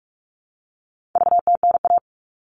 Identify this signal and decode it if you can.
Morse code, heard as a keyed tone, about 44 wpm, 700 Hz. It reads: VTNA
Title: All Cut Numbers - 4-Digit 45wpm